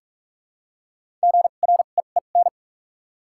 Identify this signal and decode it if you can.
Morse code, heard as a keyed tone, about 44 wpm, spaced as for 32 wpm, 700 Hz. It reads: GREEN
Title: 2x Character Spacing - Top 500 Words - 45wpm